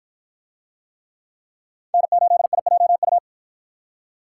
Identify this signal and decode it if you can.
Morse code, heard as a keyed tone, about 53 wpm, 700 Hz. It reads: N8IJU